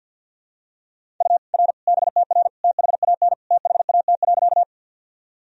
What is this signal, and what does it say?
Morse code, heard as a keyed tone, about 48 wpm, 700 Hz. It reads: U R BTR THAN THAT.
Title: Sentences from Top 400 Words and Abbreviations 50wpm